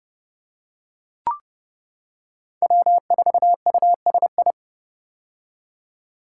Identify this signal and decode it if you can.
Morse code, heard as a keyed tone, about 28 wpm, 700 Hz. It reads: W4USI
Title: K1USN SST QSO Practice - 30wpm